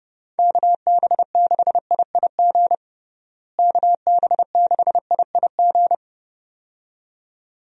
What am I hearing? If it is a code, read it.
Morse code, heard as a keyed tone, about 30 wpm, 700 Hz. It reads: KB6IIG KB6IIG
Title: USA Call Signs - 6 Characters - Repeated 2x - 30wpm